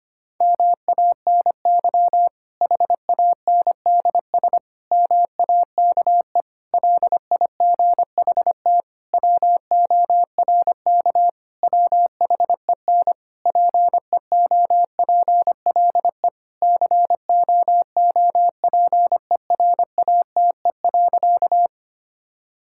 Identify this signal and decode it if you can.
Morse code, heard as a keyed tone, about 25 wpm, 700 Hz. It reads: MANY HANDS MAKE LIGHT WORK WHEN PEOPLE COOPERATE.
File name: practice / 25wpm / Difficult Sentences with Hints - 25wpm